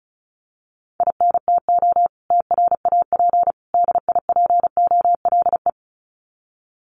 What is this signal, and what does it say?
Morse code, heard as a keyed tone, about 34 wpm, 700 Hz. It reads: INTO TRAP DIPOLE
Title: Common QSO Phrases 35wpm